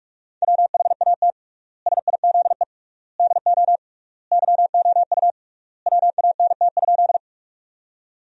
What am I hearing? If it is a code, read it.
Morse code, heard as a keyed tone, about 42 wpm, 700 Hz. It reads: WHAT SIZE DO YOU WANT?
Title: 3x Word-Spacing - Sentences from Top 400 Words - 45wpm